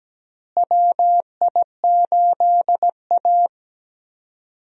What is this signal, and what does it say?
Morse code, heard as a keyed tone, about 17 wpm, 700 Hz. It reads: WI8A
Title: USA Call Signs - 4 Characters - Repeated 3x - 17wpm